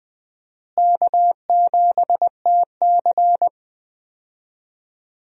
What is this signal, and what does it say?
Morse code, heard as a keyed tone, about 20 wpm, 700 Hz. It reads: K7TC
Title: USA Call Signs - 4 Characters - Repeated 2x - 20wpm